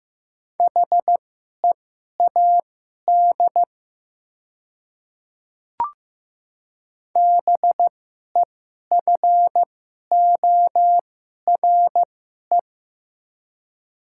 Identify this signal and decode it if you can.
Morse code, heard as a keyed tone, about 15 wpm, spaced as for 11 wpm, 690 Hz. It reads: HEAD BEFORE
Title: 2x Character Spacing - Top 500 Words - 15wpm